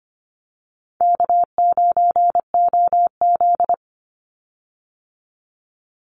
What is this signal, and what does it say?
Morse code, heard as a keyed tone, about 25 wpm, 700 Hz. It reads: K9OZ
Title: Easy Call Signs - 25wpm